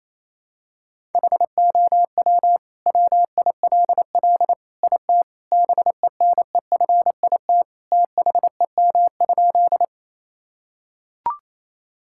Morse code, heard as a keyed tone, about 28 wpm, 700 Hz. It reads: HOW WILL IT BENEFIT THEM?
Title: Sentences from Top 1000 Words - Review Part 6 - 28wpm